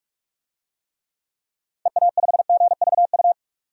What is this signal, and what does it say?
Morse code, heard as a keyed tone, about 46 wpm, 700 Hz. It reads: EA5GVU
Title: DXCC-100 Call Signs - 6 Characters - 45wpm